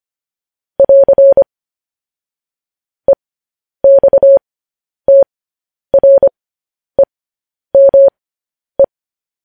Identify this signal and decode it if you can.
Morse code, heard as a keyed tone, about 25 wpm, spaced as for 10 wpm, 550 Hz. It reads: <AR> EXTREME
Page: News Headlines in Morse Code at 10 WPM